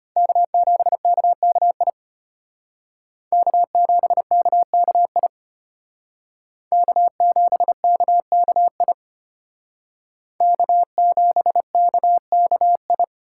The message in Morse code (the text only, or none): K7KKI K7KKI K7KKI K7KKI